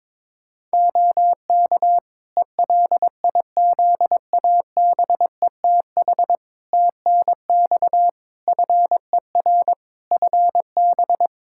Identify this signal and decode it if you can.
Morse code, heard as a keyed tone, about 22 wpm, 700 Hz. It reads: OK ELIZABETH TNX FER FB
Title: Practice QSO - Exchange 4 - Ending - 22wpm